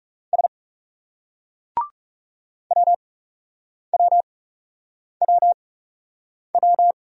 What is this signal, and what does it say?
Morse code, heard as a keyed tone, about 40 wpm, spaced as for 8 wpm, 700 Hz. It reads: S WWWW